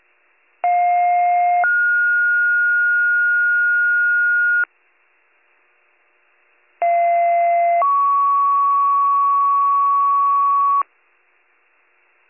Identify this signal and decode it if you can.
Two-tone page, heard as a one-second tone, then a three-second tone, about 710 then 1430 Hz, about 690 then 1120 Hz. It is heard from 0.6 to 10.8 s.